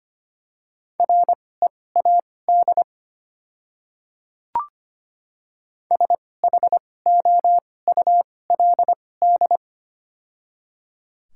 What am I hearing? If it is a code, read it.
Morse code, heard as a keyed tone, about 25 wpm, spaced as for 18 wpm, 700 Hz. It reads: READ SHOULD
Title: 2x Character Spacing - Top 300 Words - 25wpm